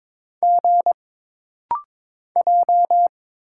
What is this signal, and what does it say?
Morse code, heard as a keyed tone, about 22 wpm, 700 Hz. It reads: G J
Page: This rapid-fire timing is geared to push your ICR (Instant Character Recognition) skills.